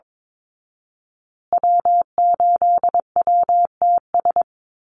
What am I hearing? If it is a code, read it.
Morse code, heard as a keyed tone, about 22 wpm, 700 Hz. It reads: W8WTS
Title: Call Signs Repeated 2x - 22wpm